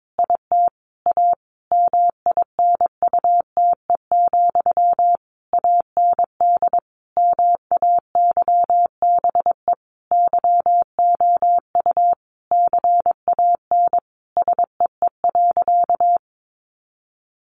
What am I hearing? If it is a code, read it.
Morse code, heard as a keyed tone, about 22 wpm, 700 Hz. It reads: IT A MINUTE, AND MAYBE YOU CAN SEE.
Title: Sentences from Top 1000 Words 22wpm